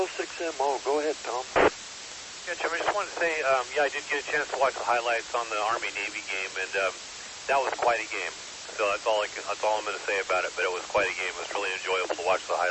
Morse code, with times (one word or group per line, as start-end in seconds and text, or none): none